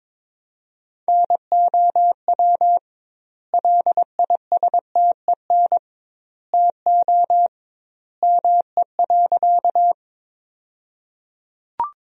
Morse code, heard as a keyed tone, about 22 wpm, 700 Hz. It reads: NOW LISTEN TO ME.
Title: 2x Word-Spacing - Sentences from Top 400 Words - 22wpm